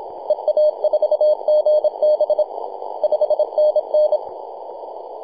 Morse code, heard as a keyed tone, about 27 wpm, 610 Hz, with radio noise. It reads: EA4GB 5NN
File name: t88zo-28-008mhzcw